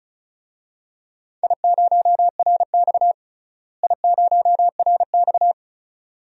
Morse code, heard as a keyed tone, about 34 wpm, 700 Hz. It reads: I0RX I0RX
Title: DXCC-100 Call Signs - 4 Characters - Repeated 2x - 35wpm